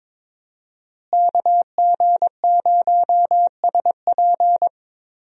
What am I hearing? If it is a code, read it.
Morse code, heard as a keyed tone, about 22 wpm, 700 Hz. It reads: KG0SP
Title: USA Call Signs - 5 Characters - Repeated 3x - 22wpm